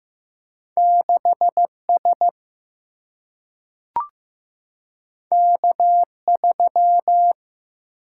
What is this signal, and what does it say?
Morse code, heard as a keyed tone, about 15 wpm, 700 Hz. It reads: 6 S K 3
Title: Single-Letter Single-Number 15wpm